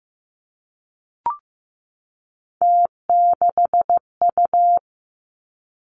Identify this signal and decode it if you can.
Morse code, heard as a keyed tone, about 15 wpm, 700 Hz. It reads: T6U
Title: Three Challenging Letter-Number Combinations 15wpm